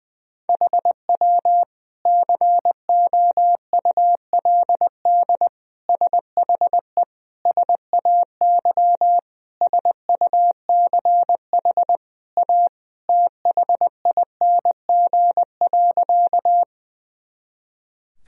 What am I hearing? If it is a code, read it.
Morse code, heard as a keyed tone, about 20 wpm, 700 Hz. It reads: HW COULD SHE SAY SUCH A THING.
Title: Sentences from Top 200 Words and Abbreviations 20wpm